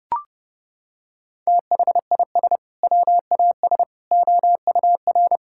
Morse code, heard as a keyed tone, about 30 wpm, 700 Hz. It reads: THIS WAS OUR
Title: Sentences from Top 1000 Words - Review Part 5 - 30wpm